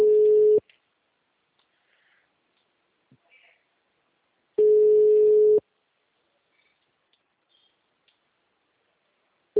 Ringback tone (Europe, 425 Hz).